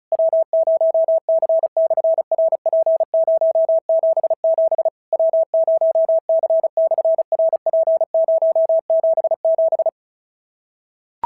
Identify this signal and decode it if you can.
Morse code, heard as a keyed tone, about 34 wpm, 650 Hz. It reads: W0C/RP077 W0C/RP077
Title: SOTA QSO Practice 35wpm